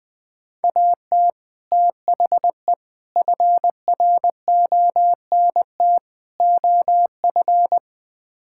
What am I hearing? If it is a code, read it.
Morse code, heard as a keyed tone, about 20 wpm, 700 Hz. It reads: AT THE FRONT OF
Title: Build A Phrase - 5-Word Phrases - 20wpm